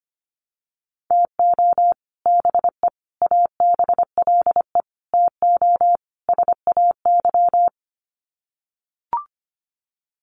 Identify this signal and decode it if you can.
Morse code, heard as a keyed tone, about 25 wpm, 700 Hz. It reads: TO BE ABLE TO SAY
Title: Sets of 5 Words Encore2 25wpm